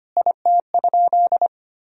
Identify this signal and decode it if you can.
Morse code, heard as a keyed tone, about 25 wpm, 700 Hz. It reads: IT?
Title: Sentences from Top 1000 Words - Review Part 3 - 25wpm